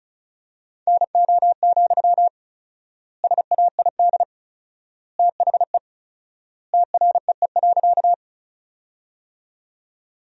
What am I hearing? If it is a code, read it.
Morse code, heard as a keyed tone, about 36 wpm, 700 Hz. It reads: NO, SAID THE TREE.
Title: 4x Word-Spacing - Sentences from Top 300 Words - 35wpm